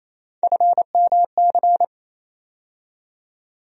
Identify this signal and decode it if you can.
Morse code, heard as a keyed tone, about 28 wpm, 700 Hz. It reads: FMC
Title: USA Call Sign Postfixes - 3 Letters 28wpm